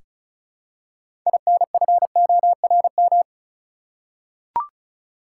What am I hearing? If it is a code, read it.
Morse code, heard as a keyed tone, about 34 wpm, 700 Hz. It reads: INFORM
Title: Six Letter Words - 35wpm